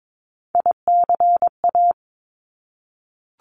Morse code, heard as a keyed tone, about 22 wpm, 700 Hz. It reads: ICA
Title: Top Three Letter Combinations in Words 22wpm